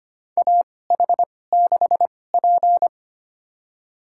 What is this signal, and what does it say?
Morse code, heard as a keyed tone, about 25 wpm, spaced as for 18 wpm, 700 Hz. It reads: AH6P
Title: USA Call Signs - 4 Characters - 2x Character Spacing - 25wpm